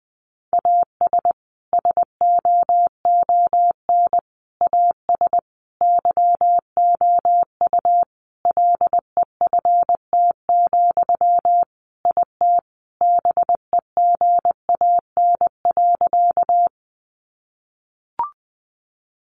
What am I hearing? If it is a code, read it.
Morse code, heard as a keyed tone, about 19 wpm, 700 Hz. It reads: AS SOON AS YOU LEFT, IT BEGAN.
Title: Sentences from Top 400 Words 20wpm